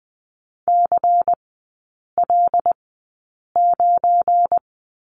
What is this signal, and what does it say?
Morse code, heard as a keyed tone, about 20 wpm, 700 Hz. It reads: C L 9
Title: 205 Lesson 33e - Letter j - Three Characters 20wpm